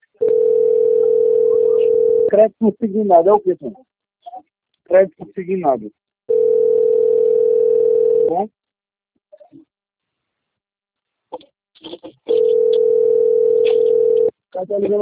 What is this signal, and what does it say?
Ringback tone (North America), 440 + 480 Hz, with speech.
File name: out- - - - - .6429